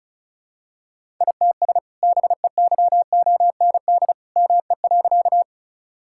Morse code, heard as a keyed tone, about 34 wpm, 700 Hz. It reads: ITS BEYOND ME.
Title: Sentences from Top 1000 Words - Review Part 6 - 35wpm